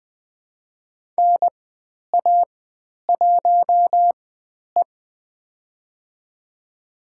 Morse code, heard as a keyed tone, about 20 wpm, spaced as for 10 wpm, 700 Hz. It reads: NA1E
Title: 095 Lesson 16g - Letter c - Call Signs 20wpm